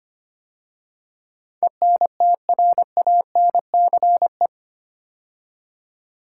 Morse code, heard as a keyed tone, about 25 wpm, 690 Hz. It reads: ENTRANCE
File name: 8 Letter Words - First 3 Letters Given - 25wpm